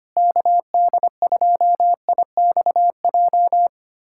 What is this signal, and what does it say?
Morse code, heard as a keyed tone, about 25 wpm, 700 Hz. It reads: KD2IXJ
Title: USA Callsign DE Callsign - 6 Characters - 25wpm